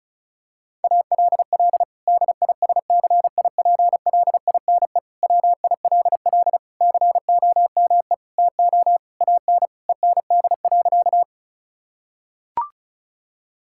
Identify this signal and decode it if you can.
Morse code, heard as a keyed tone, about 34 wpm, 700 Hz. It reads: ALL DISCIPLINE WILL COME TO AN END.
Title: Sentences from Top 1000 Words - Review Part 6 - 35wpm